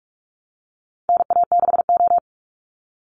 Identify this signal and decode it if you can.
Morse code, heard as a keyed tone, about 44 wpm, 700 Hz. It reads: NA6O
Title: NA SOTA Chaser Callsigns 45wpm